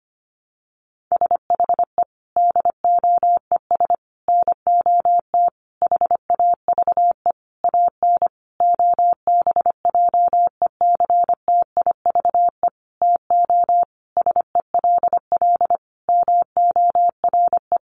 Morse code, heard as a keyed tone, about 25 wpm, 700 Hz. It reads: SHE DOES NOT HAVE AN OBJECTIVE TO SELL MORE
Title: Sentences from Top 1200 Words 25wpm